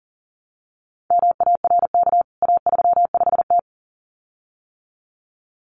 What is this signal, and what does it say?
Morse code, heard as a keyed tone, about 40 wpm, 700 Hz. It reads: MARK A35T
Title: CWT Exchange 40wpm